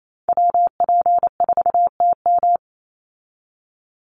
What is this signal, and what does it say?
Morse code, heard as a keyed tone, about 28 wpm, 700 Hz. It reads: WP4TM
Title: USA Call Signs 28wpm